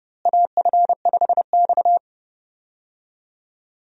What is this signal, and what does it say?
Morse code, heard as a keyed tone, about 32 wpm, 700 Hz. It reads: AF5X
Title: USA Call Signs - 4 Characters - Repeated 3x - 30wpm